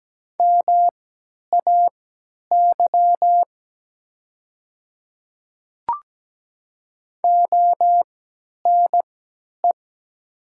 Morse code, heard as a keyed tone, about 17 wpm, spaced as for 10 wpm, 700 Hz. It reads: MAY ONE
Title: 3x Character Spacing - Top 100 Words - 17wpm